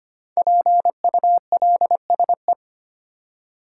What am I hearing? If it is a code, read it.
Morse code, heard as a keyed tone, about 25 wpm, 700 Hz. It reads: PULSE
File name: 5 Letter Words - First 3 Letters Given - 25wpm